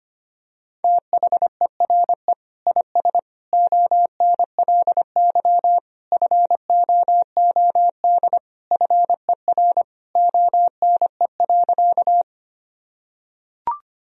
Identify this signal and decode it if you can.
Morse code, heard as a keyed tone, about 25 wpm, 700 Hz. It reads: THERE IS ONLY FOOD FER ONE.
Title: Sentences from Top 300 Words and Abbreviations 25wpm